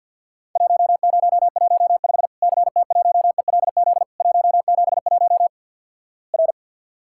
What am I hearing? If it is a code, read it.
Morse code, heard as a keyed tone, about 50 wpm, 700 Hz. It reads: 1015 CT1ELZ 171 R
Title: Worked All Europe DX Contest - QTC Exchange with Repeats - 50wpm